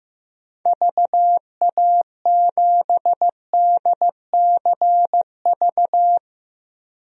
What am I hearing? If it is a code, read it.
Morse code, heard as a keyed tone, about 15 wpm, 700 Hz. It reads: VA7DCV
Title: DXCC-100 Call Signs - 6 Characters - 15wpm